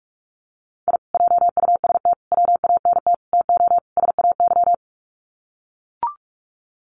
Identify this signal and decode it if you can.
Morse code, heard as a keyed tone, about 44 wpm, 700 Hz. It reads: I JUST WANT TO SAY
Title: Sets of 5 Words Encore 45wpm